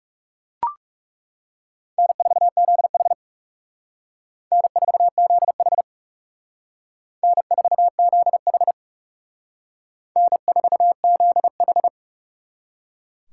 Morse code, heard as a keyed tone, about 37 wpm, 700 Hz. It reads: N4ZH N4ZH N4ZH N4ZH
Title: Speed Racing - USA Call Signs 30wpm